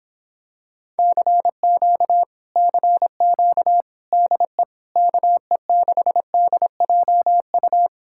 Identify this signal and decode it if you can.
Morse code, heard as a keyed tone, about 26 wpm, 700 Hz. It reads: CQ CQ DE KE6DJU